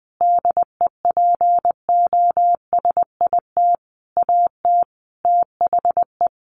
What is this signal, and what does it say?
Morse code, heard as a keyed tone, about 20 wpm, 700 Hz. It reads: DEPOSIT AT THE
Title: Sentences from Top 1400 Words 20wpm